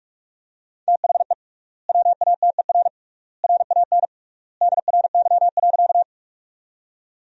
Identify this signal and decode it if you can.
Morse code, heard as a keyed tone, about 41 wpm, 700 Hz. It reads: THE WATER RAN DRY.
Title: 3x Word-Spacing - Sentences from Top 500 Words - 45wpm